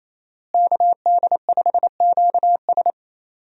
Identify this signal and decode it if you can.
Morse code, heard as a keyed tone, about 28 wpm, 700 Hz. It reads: KD5QS
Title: USA Call Signs - 5 Characters - Repeated 2x - 28wpm